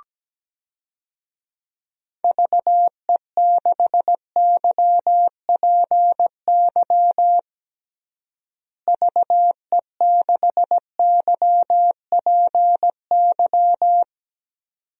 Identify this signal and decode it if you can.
Morse code, heard as a keyed tone, about 17 wpm, 700 Hz. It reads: VE6YPY VE6YPY
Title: DXCC-100 Call Signs - 6 Characters - Repeated 2x - 17wpm